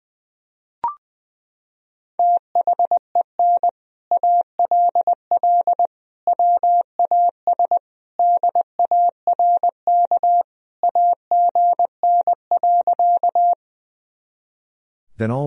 Morse code, heard as a keyed tone, about 20 wpm, 700 Hz. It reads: THEN ALL WAS DARK AGN.
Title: Sentences from Top 500 Words and Abbreviations 20wpm